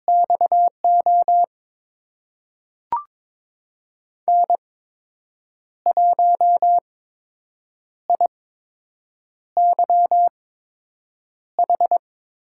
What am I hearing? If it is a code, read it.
Morse code, heard as a keyed tone, about 23 wpm, spaced as for 7 wpm, 700 Hz. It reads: XO N1IYH
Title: USA Call Signs - 5 Characters - 8x Character Spacing - 22wpm